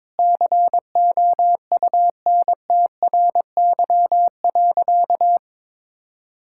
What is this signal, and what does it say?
Morse code, heard as a keyed tone, about 22 wpm, 700 Hz. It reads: COUNTRY.
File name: practice / 22wpm / Sentences with 4-Word Phrases - 22wpm